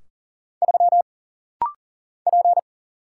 Morse code, heard as a keyed tone, about 40 wpm, 700 Hz. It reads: 3 P
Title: Single Letter-Number - Rapid-Fire 40wpm